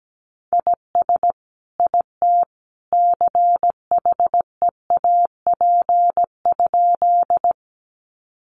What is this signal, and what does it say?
Morse code, heard as a keyed tone, about 17 wpm, 700 Hz. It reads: IS IT CHEAP?
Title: Sentences from Top 1200 Words 17wpm